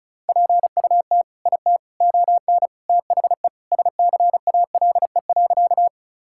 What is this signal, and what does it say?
Morse code, heard as a keyed tone, about 34 wpm, 700 Hz. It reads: PUT IT ON THE SCALE.
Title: Sentences from Top 1000 Words - Review Part 5 - 35wpm